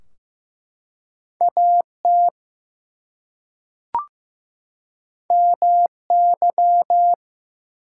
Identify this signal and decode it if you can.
Morse code, heard as a keyed tone, about 15 wpm, 700 Hz. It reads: AT MY
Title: Two Letter Words - 15wpm